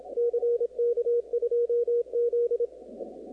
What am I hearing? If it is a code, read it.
Morse code, heard as a keyed tone, about 27 wpm, 500 Hz, with radio noise. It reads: CK2Z
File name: contests / 2019-cw-wpx-kt5j-k5pi- -2330 0-38